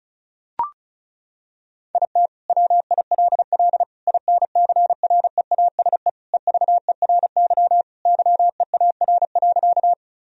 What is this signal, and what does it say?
Morse code, heard as a keyed tone, about 34 wpm, 700 Hz. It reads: IT WILL INCREASE EVERY YEAR.
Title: Sentences from Top 1000 Words - Review Part 4 - 35wpm